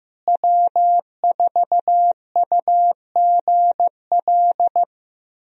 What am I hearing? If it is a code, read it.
Morse code, heard as a keyed tone, about 15 wpm, 700 Hz. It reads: W4UGL
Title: USA Call Signs - 5 Characters - Repeated 3x - 15wpm